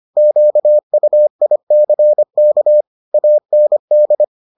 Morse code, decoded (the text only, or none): QUICK AND